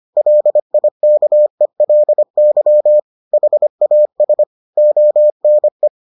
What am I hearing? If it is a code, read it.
Morse code, heard as a keyed tone, about 25 wpm, 600 Hz. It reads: LIKELY HAS ONE